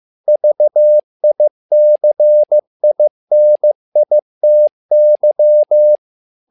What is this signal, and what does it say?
Morse code, heard as a keyed tone, about 15 wpm, 600 Hz. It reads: VICINITY
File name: learning center / tower trouble / tower trouble 15wpm